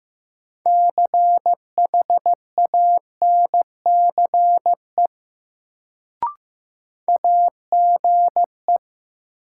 Morse code, heard as a keyed tone, about 15 wpm, 700 Hz. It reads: CHANCE AGE
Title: Sets of 3 Words - New Word Drill - 15wpm